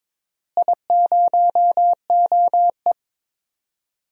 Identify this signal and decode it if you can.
Morse code, heard as a keyed tone, about 22 wpm, 700 Hz. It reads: I0OE
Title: DXCC-100 Call Signs - 4 Characters - Repeated 2x - 22wpm